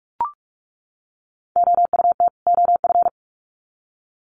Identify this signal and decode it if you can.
Morse code, heard as a keyed tone, about 44 wpm, 700 Hz. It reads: OUT OF
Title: Sets of 2 Words 45wpm